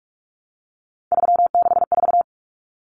Morse code, heard as a keyed tone, about 44 wpm, 700 Hz. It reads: 364
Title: Common Cut Numbers - 3-Digit 45wpm